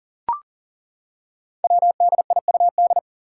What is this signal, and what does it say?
Morse code, heard as a keyed tone, about 38 wpm, 700 Hz. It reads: WDIUD
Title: Code Groups - 5 Letter:Numbers - 40wpm